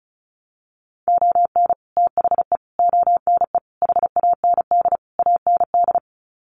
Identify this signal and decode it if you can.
Morse code, heard as a keyed tone, about 34 wpm, 700 Hz. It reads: ON THE ONE HAND AND
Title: Sets of 5 Words Encore2 35wpm